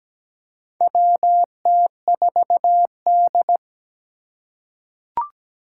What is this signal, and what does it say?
Morse code, heard as a keyed tone, about 17 wpm, 700 Hz. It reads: WT4D
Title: USA Call Signs - 4 Characters - 4x Character Spacing - 17wpm